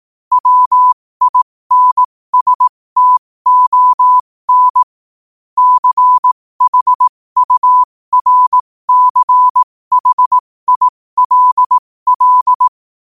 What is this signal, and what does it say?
Morse code, heard as a keyed tone, about 18 wpm, spaced as for 15 wpm, 1000 Hz. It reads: WINSTON CHURCHILL
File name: qotdfiles / 15wpm